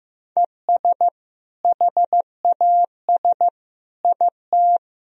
Morse code, heard as a keyed tone, about 15 wpm, 700 Hz. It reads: ES HAS IT
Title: Sentences from Top 100 Words and Abbreviations 15wpm